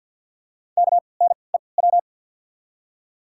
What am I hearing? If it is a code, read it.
Morse code, heard as a keyed tone, about 53 wpm, spaced as for 29 wpm, 700 Hz. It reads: KNEW
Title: 3x Character Spacing - Top 500 Words - 50wpm